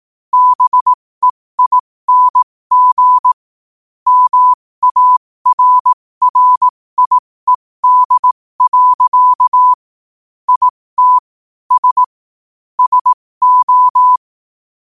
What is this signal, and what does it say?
Morse code, heard as a keyed tone, about 18 wpm, spaced as for 15 wpm, 1000 Hz. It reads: BEING MARRIED. IT S SO
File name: 15wpm